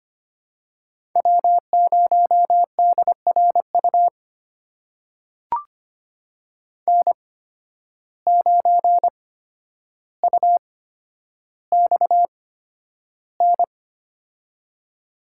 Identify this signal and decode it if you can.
Morse code, heard as a keyed tone, about 25 wpm, spaced as for 11 wpm, 700 Hz. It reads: W0DRU N9UXN
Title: USA Call Signs - 8x Character Spacing - 25wpm